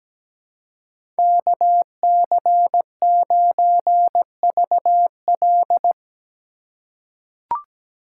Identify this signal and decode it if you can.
Morse code, heard as a keyed tone, about 17 wpm, 700 Hz. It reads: KC9VL
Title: USA Call Signs - 5 Characters - 2x Character Spacing - 17wpm